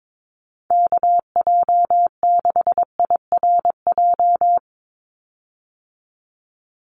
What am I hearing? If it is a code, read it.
Morse code, heard as a keyed tone, about 22 wpm, 700 Hz. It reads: KJ6IRJ
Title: USA Call Signs 22wpm